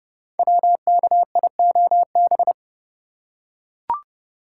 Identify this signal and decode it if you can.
Morse code, heard as a keyed tone, about 31 wpm, 700 Hz. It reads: WKIOB
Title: Code Groups - 5 Letters - 30wpm